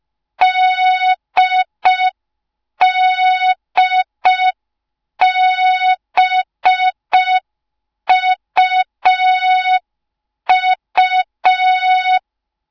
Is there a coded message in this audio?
Yes: Morse code from 0.4 to 12.2 s, heard as a keyed tone, about 5 wpm, 3700 Hz. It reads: DDBUU